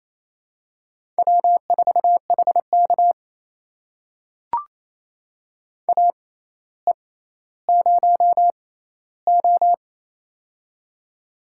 Morse code, heard as a keyed tone, about 27 wpm, spaced as for 14 wpm, 700 Hz. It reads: W4HK AE0O
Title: USA Call Signs - 4 Characters - 6x Character Spacing - 28wpm